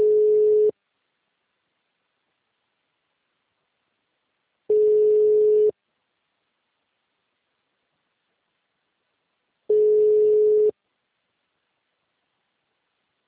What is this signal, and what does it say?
Ringback tone (Europe, 425 Hz).